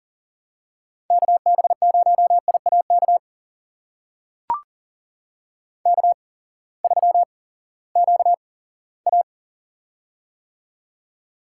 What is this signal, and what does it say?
Morse code, heard as a keyed tone, about 39 wpm, spaced as for 20 wpm, 700 Hz. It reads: KB0IAK K3QA
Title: USA Call Signs - 8x Character Spacing - 40wpm